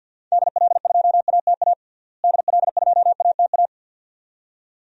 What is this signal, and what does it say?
Morse code, heard as a keyed tone, about 52 wpm, 700 Hz. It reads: DL2ATA DL2ATA
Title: DXCC-100 Call Signs - 6 Characters - Repeated 2x - 50wpm